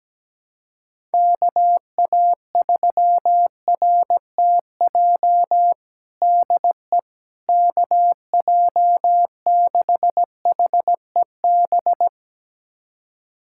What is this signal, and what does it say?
Morse code, heard as a keyed tone, about 17 wpm, 700 Hz. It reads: KA3RTJ DE KJ6HEB
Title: USA Callsign DE Callsign 17wpm